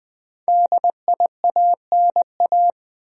Morse code, heard as a keyed tone, about 21 wpm, 700 Hz. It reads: DIANA
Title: USA Common Names - Max 5 Letters - 20wpm